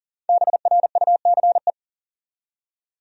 Morse code, heard as a keyed tone, about 40 wpm, 700 Hz. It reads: BRUCE
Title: USA Common Names - Max 5 Letters - 40wpm